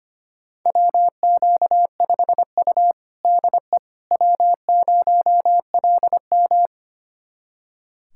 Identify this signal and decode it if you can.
Morse code, heard as a keyed tone, about 25 wpm, 700 Hz. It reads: WQ5U DE W0LM
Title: USA Callsign DE Callsign - 4 Characters - 25wpm